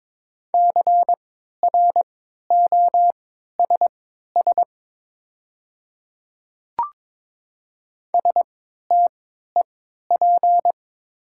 Morse code, heard as a keyed tone, about 22 wpm, spaced as for 13 wpm, 700 Hz. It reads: CROSS STEP
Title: 3x Character Spacing - Top 500 Words - 22wpm